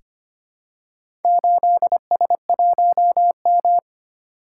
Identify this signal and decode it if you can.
Morse code, heard as a keyed tone, about 25 wpm, 700 Hz. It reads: 8S1M
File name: practice / 25wpm / Four Letter-Number Combinations - 25wpm